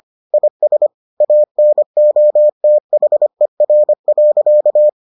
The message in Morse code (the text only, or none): IS ANOTHER.